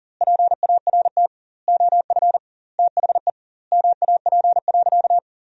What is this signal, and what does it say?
Morse code, heard as a keyed tone, about 40 wpm, 700 Hz. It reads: PART OF THE MAP.
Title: 2x Word-Spacing - Sentences from Top 400 Words - 40wpm